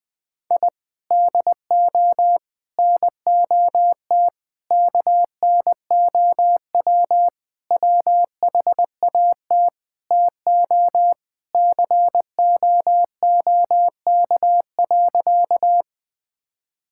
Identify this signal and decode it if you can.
Morse code, heard as a keyed tone, about 20 wpm, 700 Hz. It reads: I DO NOT KNOW WHAT TO COOK.
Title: Sentences with 5-Word Phrases - Spoken Beginning - 20wpm